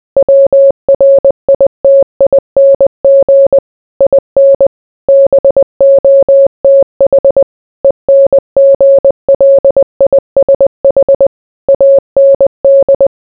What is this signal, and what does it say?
Morse code, heard as a keyed tone, about 20 wpm, 550 Hz. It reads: WRITING IN BOTH ENGLISH AND